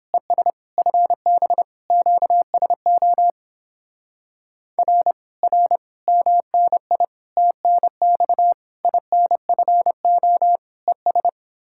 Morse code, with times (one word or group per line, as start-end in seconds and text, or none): 0.1-0.5 ES
0.8-1.6 FB
1.9-3.3 QSO
4.8-5.1 R
5.4-5.8 R
6.1-7.0 MNI
7.4-8.5 TNX
8.8-10.6 INFO
10.9-11.3 ES